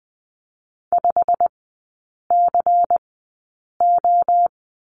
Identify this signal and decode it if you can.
Morse code, heard as a keyed tone, about 20 wpm, spaced as for 8 wpm, 700 Hz. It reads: 5CO
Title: 172 Lesson 28e - Number 9 - Three Characters 20wpm